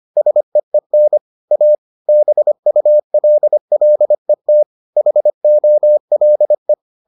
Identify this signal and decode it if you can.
Morse code, heard as a keyed tone, about 25 wpm, 600 Hz. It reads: SEEN A BULLET HOLE